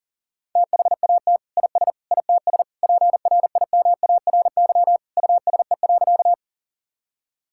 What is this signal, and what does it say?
Morse code, heard as a keyed tone, about 40 wpm, 700 Hz. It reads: THAT IS ITS PRIMARY USE.
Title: Sentences from Top 1000 Words - Review Part 5 - 40wpm